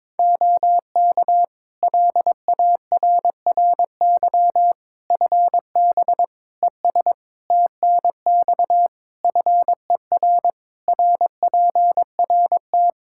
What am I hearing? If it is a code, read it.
Morse code, heard as a keyed tone, about 22 wpm, 700 Hz. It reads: OK LARRY FB ES TNX FER RPRT
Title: Practice QSO - Exchange 2 - Info - 22wpm